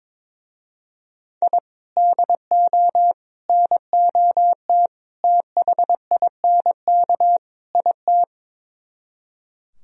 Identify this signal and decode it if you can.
Morse code, heard as a keyed tone, about 22 wpm, 700 Hz. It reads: I DO NOT THINK IT
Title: Speed-Racing - Sets of 5 Words 22wpm